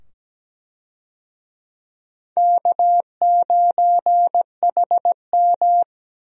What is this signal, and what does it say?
Morse code, heard as a keyed tone, about 17 wpm, 700 Hz. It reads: K9HM
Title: USA Call Signs - 4 Characters - 17wpm